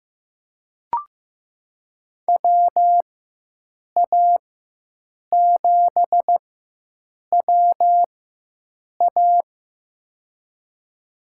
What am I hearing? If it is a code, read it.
Morse code, heard as a keyed tone, about 15 wpm, spaced as for 7 wpm, 700 Hz. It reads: WA7WA
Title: USA Call Signs - 5 Characters - 4x Character Spacing - 15wpm